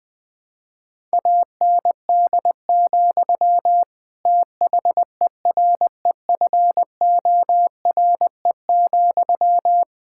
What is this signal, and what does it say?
Morse code, heard as a keyed tone, about 20 wpm, 700 Hz. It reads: AND, THEREFORE,
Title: Sentences from Top 1000 Words - Review Part 5 - 20wpm